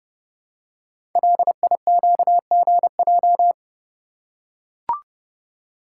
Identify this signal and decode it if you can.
Morse code, heard as a keyed tone, about 28 wpm, 700 Hz. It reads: LIQGJ
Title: Code Groups - 5 Letters - 30wpm